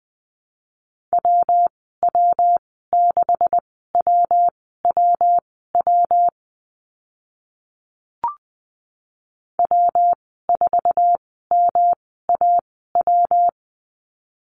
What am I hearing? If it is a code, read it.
Morse code, heard as a keyed tone, about 20 wpm, spaced as for 14 wpm, 700 Hz. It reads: WW6WWW W4MAW
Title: 140 Lesson 23g - Letter f - Call Signs 20wpm